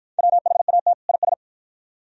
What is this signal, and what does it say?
Morse code, heard as a keyed tone, about 47 wpm, 700 Hz. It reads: WHAT IS